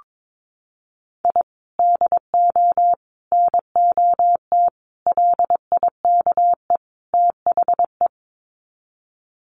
Morse code, heard as a keyed tone, about 22 wpm, 700 Hz. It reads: I DO NOT LIKE THE
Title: Sets of 5 Words Encore 22wpm